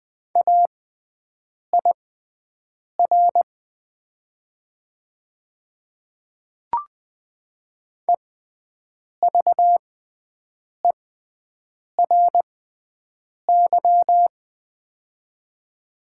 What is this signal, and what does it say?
Morse code, heard as a keyed tone, about 20 wpm, spaced as for 7 wpm, 700 Hz. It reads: AIR EVERY
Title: 6x Character Spacing - Top 300 Words - 20wpm